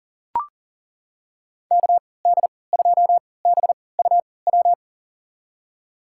Morse code, heard as a keyed tone, about 40 wpm, spaced as for 23 wpm, 700 Hz. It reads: KD2BUW
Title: USA Call Signs - 6 Characters - 3x Character Spacing - 40wpm